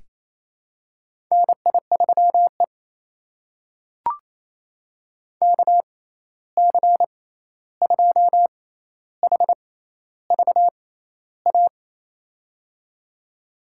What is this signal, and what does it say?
Morse code, heard as a keyed tone, about 28 wpm, spaced as for 13 wpm, 700 Hz. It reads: NI3E KC2HVA
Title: USA Call Signs - 6x Character Spacing - 28wpm